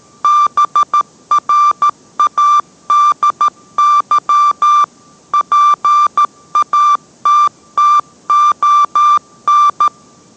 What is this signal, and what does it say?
Morse code, heard as a keyed tone, about 13 wpm, 1180 Hz. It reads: BRADY PATTON